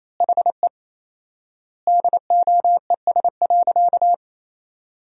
Morse code, heard as a keyed tone, about 27 wpm, 700 Hz. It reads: HE DOES.
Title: 4x Word-Spacing - Sentences from Top 500 Words - 28wpm